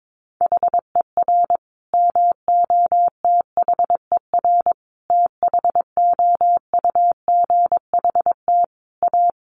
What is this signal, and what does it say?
Morse code, heard as a keyed tone, about 22 wpm, 700 Hz. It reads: HER MOTHER THOUGHT A
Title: Sentences from Top 800 Words 22wpm